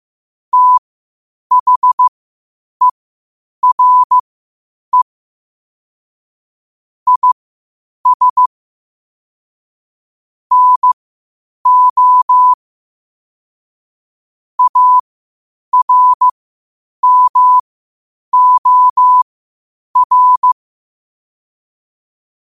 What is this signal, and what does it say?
Morse code, heard as a keyed tone, about 15 wpm, spaced as for 8 wpm, 1000 Hz. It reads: THERE IS NO ARMOR
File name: qotdfiles / 8wpm